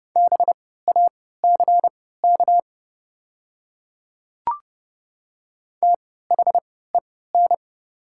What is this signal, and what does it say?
Morse code, heard as a keyed tone, about 30 wpm, spaced as for 17 wpm, 700 Hz. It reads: BACK THEN
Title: 3x Character Spacing - Top 200 Words - 30wpm